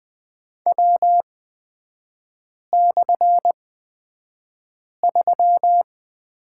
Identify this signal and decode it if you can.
Morse code, heard as a keyed tone, about 20 wpm, spaced as for 5 wpm, 700 Hz. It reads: W/3
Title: 236 Lesson 38e - Letter q - Three Characters 20wpm